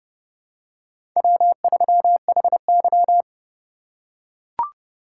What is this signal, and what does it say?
Morse code, heard as a keyed tone, about 31 wpm, 700 Hz. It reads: W3HY
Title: USA Call Signs - 4 Characters - 3x Character Spacing - 30wpm